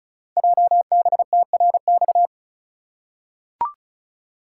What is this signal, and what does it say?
Morse code, heard as a keyed tone, about 36 wpm, 700 Hz. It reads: JBTRX
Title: Code Groups - 5 Letters - 35wpm